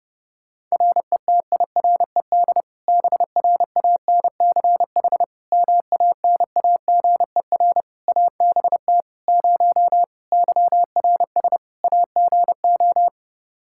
Morse code, heard as a keyed tone, about 30 wpm, 700 Hz. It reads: RETIRED BRANCH MANAGER ABT 0 YRS AGO
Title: Practice QSO - Exchange 3 - More Info - 30wpm